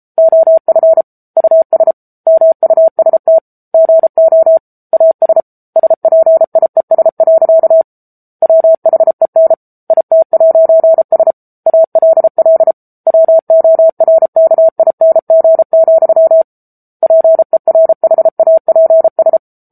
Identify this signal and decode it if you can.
Morse code, heard as a keyed tone, about 34 wpm, 650 Hz. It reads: OF US MUST GO AS SPIES. WHEN IT'S ALL WORKING, PERHAPS
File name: Book Two / 07-Chapter 7 - 33 WPM